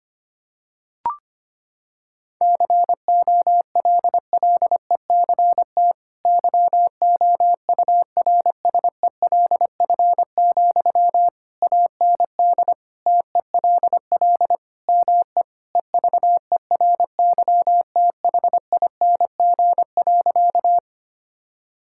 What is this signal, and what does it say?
Morse code, heard as a keyed tone, about 25 wpm, 700 Hz. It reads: COLLECT YOURSELF, AND TELL ME EVERYTHING.
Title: Sentences from Top 1600 Words 25wpm